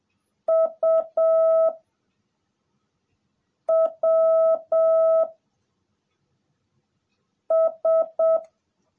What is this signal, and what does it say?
Morse code, heard as a keyed tone, about 7 wpm, 650 Hz. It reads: U W S
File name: jh8cbh / 2021cw / cw18a